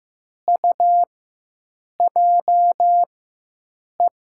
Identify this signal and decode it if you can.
Morse code, heard as a keyed tone, about 15 wpm, spaced as for 7 wpm, 700 Hz. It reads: UJE
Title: USA Call Signs - 4x Character Spacing - 15wpm